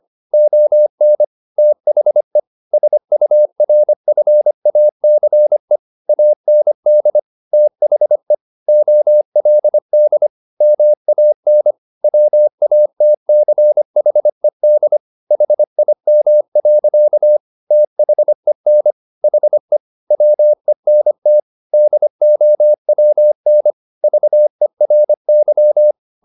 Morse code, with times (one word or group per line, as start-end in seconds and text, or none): 0.3-1.2 ON
1.6-2.4 THE
2.7-5.8 SURFACE
6.1-7.2 AND
7.5-8.3 THE
8.7-10.3 OLD
10.6-11.7 MAN
12.0-15.0 WATCHED
15.3-17.4 HIM.
17.7-18.9 THEN
19.2-19.8 HE
20.1-21.4 WENT
21.7-23.7 DOWN
24.0-25.9 VERY